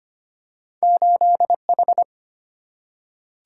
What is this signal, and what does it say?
Morse code, heard as a keyed tone, about 25 wpm, 700 Hz. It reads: 8H
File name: Two Letter-Number Combinations - 25wpm